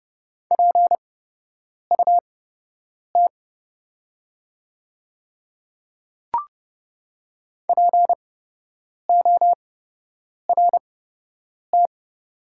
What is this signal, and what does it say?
Morse code, heard as a keyed tone, about 30 wpm, spaced as for 8 wpm, 700 Hz. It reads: PUT PORT
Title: 8x Character Spacing - Top 300 Words - 30wpm